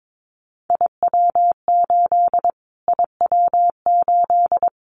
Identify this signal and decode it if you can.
Morse code, heard as a keyed tone, about 22 wpm, 700 Hz. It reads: IW8 IW8
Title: Callsign Prefixes for Top 25 DXCC - Repeated 3x - 22wpm